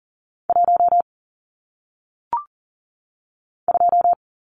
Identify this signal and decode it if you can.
Morse code, heard as a keyed tone, about 40 wpm, 700 Hz. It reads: 1 2
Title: Numbers - 1-Digit 40wpm